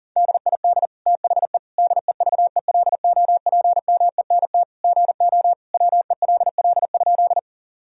Morse code, heard as a keyed tone, about 40 wpm, 700 Hz. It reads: DID THE DEVELOPMENT GO WELL?
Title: Sentences from Top 1000 Words - Review Part 4 - 40wpm